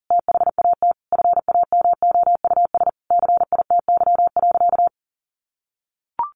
Morse code, heard as a keyed tone, about 37 wpm, 700 Hz. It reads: THAT FAMOUS CITY.
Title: Sentences from Top 1500 Words 40wpm